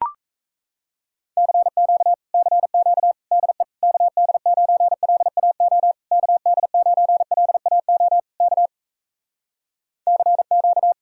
Morse code, heard as a keyed tone, about 37 wpm, 700 Hz. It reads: CQ CQ DE KD9LAO KD9LAO K CQ